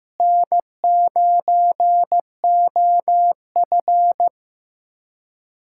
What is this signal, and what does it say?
Morse code, heard as a keyed tone, about 15 wpm, 700 Hz. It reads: N9OF
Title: USA Callsign DE Callsign - 4 Characters - 15wpm